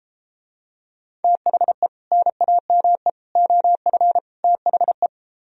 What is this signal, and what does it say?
Morse code, heard as a keyed tone, about 32 wpm, 700 Hz. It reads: THE NAME OF THE